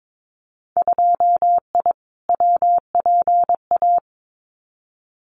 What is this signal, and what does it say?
Morse code, heard as a keyed tone, about 22 wpm, 700 Hz. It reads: 2I WPA
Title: Winter Field Day Exchanges 22wpm